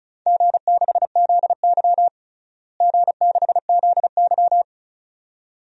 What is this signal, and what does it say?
Morse code, heard as a keyed tone, about 36 wpm, 700 Hz. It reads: G6ZY G6ZY
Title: DXCC-100 Call Signs - 4 Characters - Repeated 2x - 35wpm